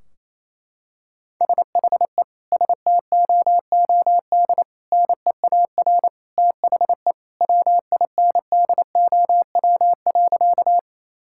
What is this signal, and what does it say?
Morse code, heard as a keyed tone, about 28 wpm, 700 Hz. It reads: SHE STOOD NEAR THE WINDOW.
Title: Sentences from Top 1600 Words 28wpm